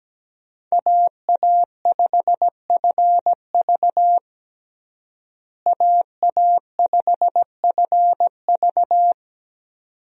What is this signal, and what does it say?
Morse code, heard as a keyed tone, about 17 wpm, 700 Hz. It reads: AA5FV AA5FV
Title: USA Call Signs - 5 Characters - Repeated 2x - 17wpm